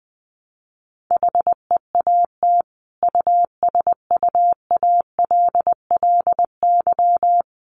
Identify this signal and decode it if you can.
Morse code, heard as a keyed tone, about 20 wpm, 700 Hz. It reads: HEAT USUALLY
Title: Sentences from Top 1500 Words 20wpm